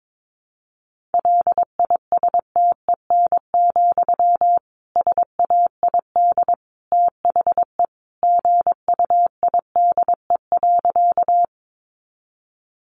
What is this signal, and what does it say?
Morse code, heard as a keyed tone, about 22 wpm, 700 Hz. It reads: LISTEN, SAID THE GUIDE.
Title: Sentences from Top 600 Words 22wpm